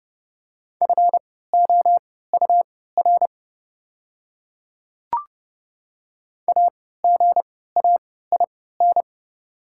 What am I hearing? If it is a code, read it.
Morse code, heard as a keyed tone, about 30 wpm, spaced as for 17 wpm, 690 Hz. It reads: FOUR AGAIN
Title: 3x Character Spacing - Top 500 Words - 30wpm